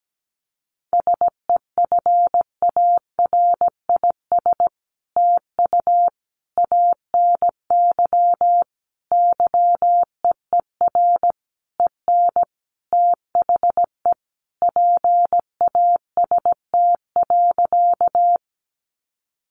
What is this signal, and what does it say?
Morse code, heard as a keyed tone, about 17 wpm, 700 Hz. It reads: SEFARIS TU ANY YEER EN THE PAST.
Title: A Sound Of Thunder 17wpm